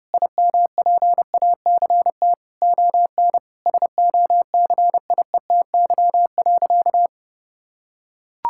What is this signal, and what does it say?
Morse code, heard as a keyed tone, about 32 wpm, 700 Hz. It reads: IMPACT ON SOCIETY.
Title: Sentences from Top 1000 Words - Review Part 6 - 30wpm